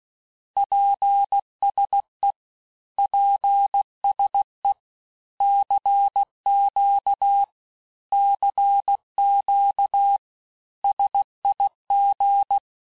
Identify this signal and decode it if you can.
Morse code, heard as a keyed tone, about 16 wpm, 800 Hz. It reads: PSE PSE CQ CQ SIG
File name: cours-du-lundi files / Cours-1586-texte-abreviations-vitesse-15mpm-puis-20mpm